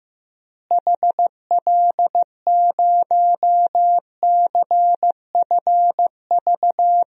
Morse code, heard as a keyed tone, about 15 wpm, 700 Hz. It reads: HL0CFV
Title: DXCC-100 Call Signs - 6 Characters - Repeated 2x - 15wpm